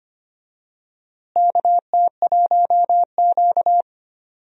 Morse code, heard as a keyed tone, about 25 wpm, 700 Hz. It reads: KT1Q
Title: USA Call Signs - 4 Characters - Repeated 3x - 25wpm